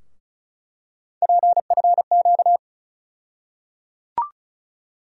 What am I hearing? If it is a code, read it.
Morse code, heard as a keyed tone, about 36 wpm, 700 Hz. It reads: PFQ
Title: USA Call Sign Postfixes - 3 Letters 35wpm